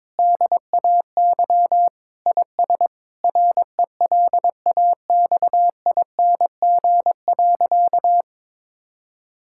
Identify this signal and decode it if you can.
Morse code, heard as a keyed tone, about 22 wpm, 700 Hz. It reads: DAY IS RELAXING.
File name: practice / 22wpm / Sentences with 3-Word Phrases - 22wpm